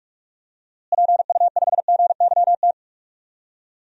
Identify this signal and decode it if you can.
Morse code, heard as a keyed tone, about 41 wpm, 700 Hz. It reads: PU5GYT
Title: DXCC-100 Call Signs - 6 Characters - Repeated 2x - 45wpm